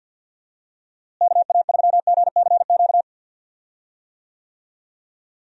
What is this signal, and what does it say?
Morse code, heard as a keyed tone, about 51 wpm, 700 Hz. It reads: KA3GCQ
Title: USA Call Signs - 6 Characters - Repeated 3x - 50wpm